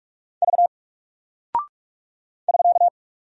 Morse code, heard as a keyed tone, about 44 wpm, 700 Hz. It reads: V <SK>
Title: Single Character (Letter:Number:Punctuation:Prosign) Alt 45wpm